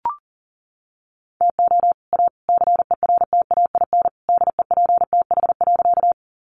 Morse code, heard as a keyed tone, about 40 wpm, 700 Hz. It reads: TO A CERTAIN DEPTH.
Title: Sentences from Top 1200 Words 40wpm